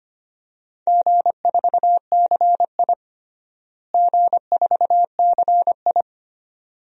Morse code, heard as a keyed tone, about 25 wpm, 700 Hz. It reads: G4CI G4CI
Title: DXCC-100 Call Signs - 4 Characters - Repeated 2x - 25wpm